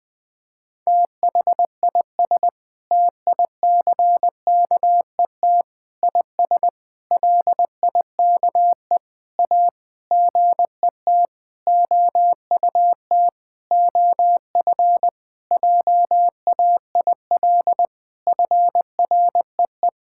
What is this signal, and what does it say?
Morse code, heard as a keyed tone, about 20 wpm, 700 Hz. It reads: THIS TICKET IS LIKE A GET OUT OF JAIL FREE
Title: Sentences with 7-Word Phrases - Spoken Beginning - 20wpm